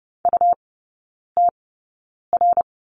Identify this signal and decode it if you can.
Morse code, heard as a keyed tone, about 30 wpm, spaced as for 9 wpm, 700 Hz. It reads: UTR
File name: Lesson 17e - Letter u - Three Characters 30wpm